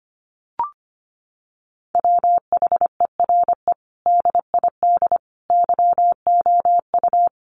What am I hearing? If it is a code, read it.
Morse code, heard as a keyed tone, about 25 wpm, 700 Hz. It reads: WHERE DID YOU
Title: Sentences from Top 200 Words 25wpm